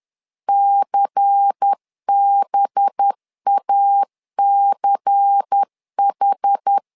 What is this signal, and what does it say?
Morse code, heard as a keyed tone, about 10 wpm, 800 Hz. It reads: CBACH
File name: LEZIONI / 5 LBVC / 5 LBVC 10wpm 2349